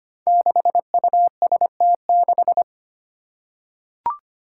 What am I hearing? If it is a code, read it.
Morse code, heard as a keyed tone, about 25 wpm, 700 Hz. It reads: 6UST6
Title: Five Challenging Letter-Number Combinations 25wpm